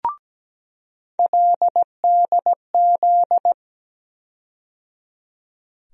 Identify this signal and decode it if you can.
Morse code, heard as a keyed tone, about 17 wpm, 700 Hz. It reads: LDZ
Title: USA Call Sign Postfixes - 3 Letters 17wpm